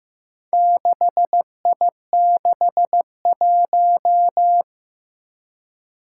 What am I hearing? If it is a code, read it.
Morse code, heard as a keyed tone, about 15 wpm, 700 Hz. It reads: 6I61
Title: Four Challenging Letter-Number Combinations 15wpm